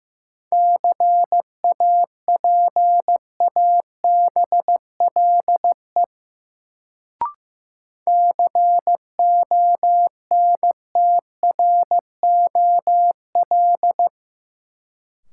Morse code, heard as a keyed tone, about 15 wpm, 700 Hz. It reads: CAPABLE CONTROL
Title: Sentences from Top 1200 Words - New Word Drill Alt - 15wpm